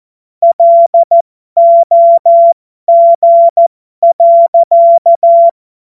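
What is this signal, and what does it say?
Morse code, heard as a keyed tone, about 14 wpm, spaced as for 12 wpm, 680 Hz. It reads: LOG.
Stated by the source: Amateur radio CW practice text made of jokes.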